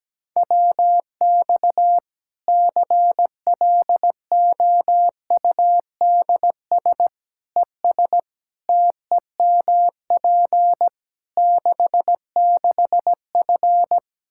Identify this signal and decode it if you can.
Morse code, heard as a keyed tone, about 17 wpm, 700 Hz. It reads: WX CLOUDS ES TEMP 66F
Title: Practice QSO - Exchange 2 - Info - 17wpm